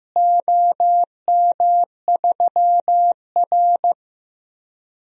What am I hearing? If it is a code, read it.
Morse code, heard as a keyed tone, about 15 wpm, 700 Hz. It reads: OM3R
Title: DXCC-100 Call Signs - 4 Characters - Repeated 2x - 15wpm